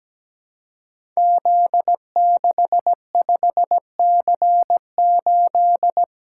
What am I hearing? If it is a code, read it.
Morse code, heard as a keyed tone, about 17 wpm, 700 Hz. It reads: Z65C8
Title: Code Groups - 5 Letter:Numbers - 17wpm